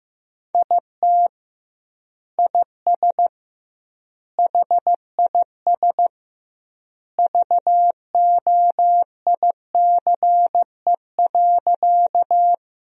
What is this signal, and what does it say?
Morse code, heard as a keyed tone, about 16 wpm, 700 Hz. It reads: IT IS HIS VOICE.
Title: 2x Word-Spacing - Sentences from Top 500 Words - 15wpm